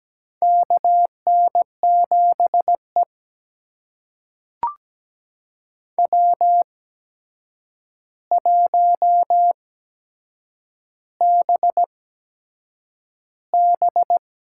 Morse code, heard as a keyed tone, about 17 wpm, spaced as for 7 wpm, 700 Hz. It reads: KN7E W1BB
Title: USA Call Signs - 4 Characters - 8x Character Spacing - 17wpm